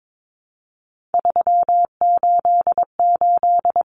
Morse code, heard as a keyed tone, about 22 wpm, 700 Hz. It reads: 388
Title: Common Cut Numbers - 3-Digit 22wpm